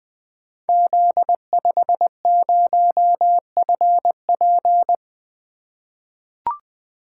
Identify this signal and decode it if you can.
Morse code, heard as a keyed tone, about 20 wpm, 700 Hz. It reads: Z50FP
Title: Code Groups - 5 Letter:Numbers - 20wpm